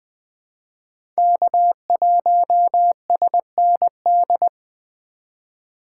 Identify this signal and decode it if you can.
Morse code, heard as a keyed tone, about 21 wpm, 700 Hz. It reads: K1SND
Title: K1USN SST QSO Practice - 20wpm